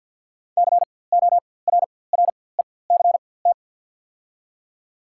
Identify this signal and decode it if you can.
Morse code, heard as a keyed tone, about 54 wpm, spaced as for 24 wpm, 690 Hz. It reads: CORRECT
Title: 4x Character Spacing - Top 500 Words - 50wpm